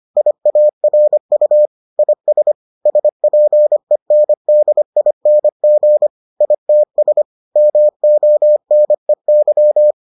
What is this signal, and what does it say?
Morse code, heard as a keyed tone, about 25 wpm, 600 Hz. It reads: IARU IS SPENDING ITS MONEY